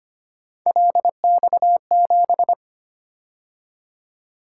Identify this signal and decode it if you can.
Morse code, heard as a keyed tone, about 29 wpm, 700 Hz. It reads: LX7
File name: Speed-Racing - Three Letter-Number Combinations - 25wpm